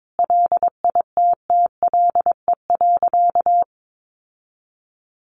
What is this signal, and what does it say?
Morse code, heard as a keyed tone, about 22 wpm, 700 Hz. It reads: LITTLE.
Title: Sentences from Top 1500 Words 22wpm